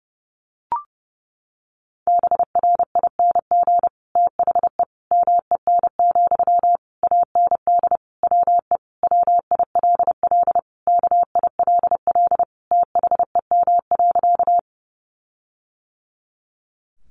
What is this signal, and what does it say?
Morse code, heard as a keyed tone, about 28 wpm, 700 Hz. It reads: BRING THE MEN, AND WE WILL KILL THEM.
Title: Sentences from Top 1400 Words 30wpm